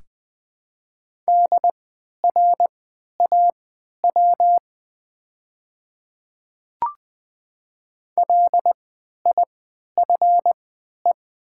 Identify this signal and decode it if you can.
Morse code, heard as a keyed tone, about 20 wpm, spaced as for 11 wpm, 700 Hz. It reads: DRAW LIFE
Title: 3x Character Spacing - Top 500 Words - 20wpm